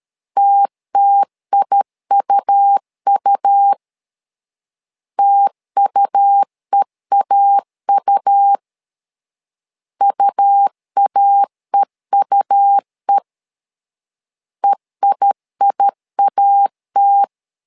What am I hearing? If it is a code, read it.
Morse code, heard as a keyed tone, about 13 wpm, 800 Hz. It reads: TTIUU TUEAU UAEUE EIIAT
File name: LEZIONI / 2 UHDN / 2 U 12wpm 1338